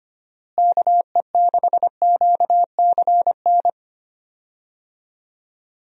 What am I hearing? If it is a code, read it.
Morse code, heard as a keyed tone, about 25 wpm, 700 Hz. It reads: KE6QCN
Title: USA Call Signs - 6 Characters - Repeated 2x - 25wpm